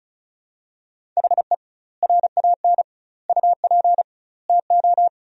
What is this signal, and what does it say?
Morse code, heard as a keyed tone, about 36 wpm, 700 Hz. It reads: HE RAN UP TO
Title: 2x Word-Spacing - Sentences from Top 500 Words - 35wpm